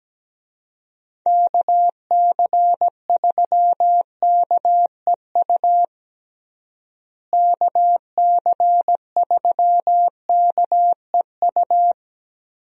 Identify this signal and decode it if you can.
Morse code, heard as a keyed tone, about 17 wpm, 700 Hz. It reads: KC3KEU KC3KEU
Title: USA Call Signs - 6 Characters - Repeated 3x - 17wpm